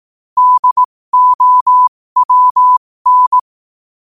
Morse code, heard as a keyed tone, about 18 wpm, spaced as for 15 wpm, 1000 Hz. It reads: DOWN
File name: qotdfiles / 15wpm